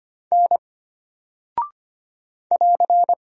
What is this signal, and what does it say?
Morse code, heard as a keyed tone, about 25 wpm, 700 Hz. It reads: N <AR>
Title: Single Character (Letter:Number:Punctuation:Prosign) Alt 25wpm